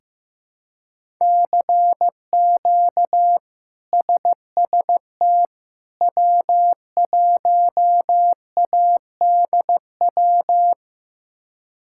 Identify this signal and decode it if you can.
Morse code, heard as a keyed tone, about 16 wpm, 700 Hz. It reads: CQ SST W1ADW
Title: K1USN SST QSO Practice - 15wpm